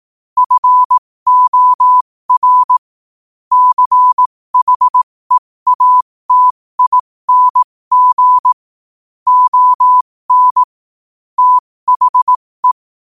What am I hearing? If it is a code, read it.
Morse code, heard as a keyed tone, about 18 wpm, spaced as for 15 wpm, 1000 Hz. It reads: FOR CHEATING ON THE
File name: qotdfiles / 15wpm